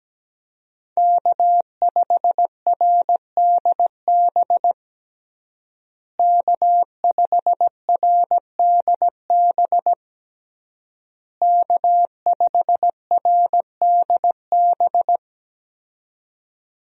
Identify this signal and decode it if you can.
Morse code, heard as a keyed tone, about 17 wpm, 700 Hz. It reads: K5RDB K5RDB K5RDB
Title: USA Call Signs - 5 Characters - Repeated 3x - 17wpm